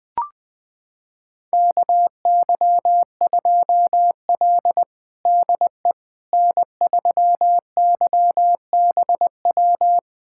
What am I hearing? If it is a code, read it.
Morse code, heard as a keyed tone, about 20 wpm, 700 Hz. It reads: KY2L DE N3YBW
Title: USA Callsign DE Callsign 20wpm